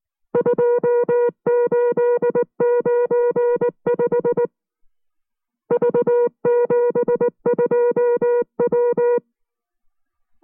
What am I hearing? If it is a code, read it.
Morse code, heard as a keyed tone, about 19 wpm, 480 Hz. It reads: 2895 V72W